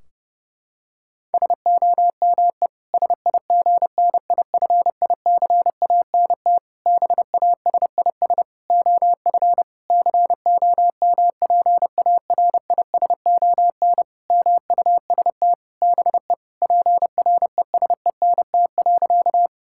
Morse code, heard as a keyed tone, about 30 wpm, 700 Hz. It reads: SOME SIGNIFICANT BASIS OF COMPARISON MUST BE PRESENT.
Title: Sentences from Top 1600 Words 30wpm